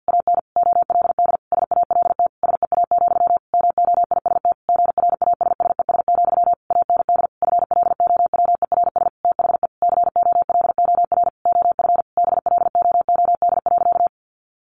Morse code, heard as a keyed tone, about 50 wpm, 700 Hz. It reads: AN OLD SALT SEA, MOIST GRASSES, AND FLOWERS THE COLOR OF BLOOD.
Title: A Sound Of Thunder 50wpm